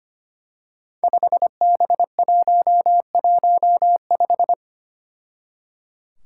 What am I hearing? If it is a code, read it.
Morse code, heard as a keyed tone, about 25 wpm, 700 Hz. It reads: 5B115
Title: Five Challenging Letter-Number Combinations 25wpm